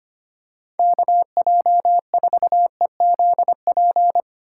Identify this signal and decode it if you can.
Morse code, heard as a keyed tone, about 25 wpm, 700 Hz. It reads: KJ4EZP
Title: USA Call Signs - 6 Characters - 4x Character Spacing - 25wpm